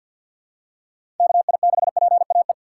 Morse code, heard as a keyed tone, about 46 wpm, 700 Hz. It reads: KI6PAE
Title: USA Call Signs - 6 Characters - 3x Character Spacing - 50wpm